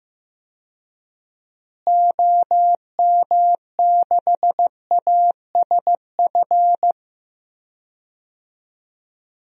Morse code, heard as a keyed tone, about 15 wpm, 700 Hz. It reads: OM6ASF
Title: DXCC-100 Call Signs - 6 Characters - Repeated 3x - 15wpm